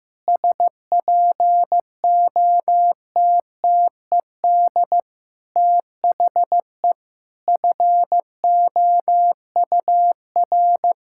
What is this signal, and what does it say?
Morse code, heard as a keyed tone, about 15 wpm, 700 Hz. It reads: SPOTTED THE FOUR
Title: Pangram Challenge v2 - 15wpm